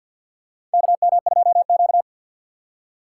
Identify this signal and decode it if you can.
Morse code, heard as a keyed tone, about 47 wpm, 700 Hz. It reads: KM2Q
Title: USA Call Signs - 4 Characters - 6x Character Spacing - 50wpm